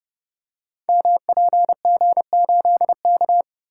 Morse code, heard as a keyed tone, about 28 wpm, 700 Hz. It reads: MPG8K
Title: Code Groups - 5 Letter:Numbers - 30wpm